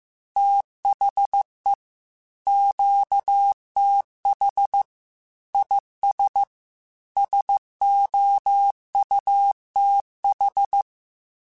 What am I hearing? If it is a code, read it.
Morse code, heard as a keyed tone, about 15 wpm, 800 Hz. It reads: THE QTH IS SOUTH
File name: QSOs / QSOs-mp3s / WA7QOX 2@13